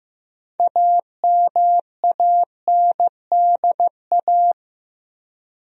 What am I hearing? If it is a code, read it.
Morse code, heard as a keyed tone, about 15 wpm, 700 Hz. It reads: AMANDA
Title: USA Common Names - 15wpm